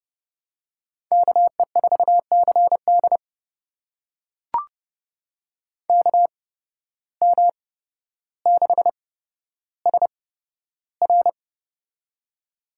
Morse code, heard as a keyed tone, about 32 wpm, spaced as for 13 wpm, 700 Hz. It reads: KE4CD KM6SR
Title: USA Call Signs - 5 Characters - 8x Character Spacing - 30wpm